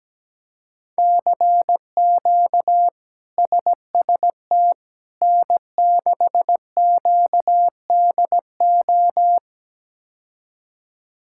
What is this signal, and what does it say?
Morse code, heard as a keyed tone, about 17 wpm, 700 Hz. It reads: CQ SST N6QDO
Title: K1USN SST QSO Practice - 17wpm